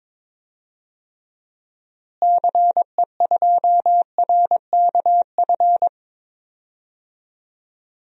Morse code, heard as a keyed tone, about 21 wpm, 700 Hz. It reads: CE2RKF
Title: DXCC-100 Call Signs - 6 Characters - Repeated 2x - 22wpm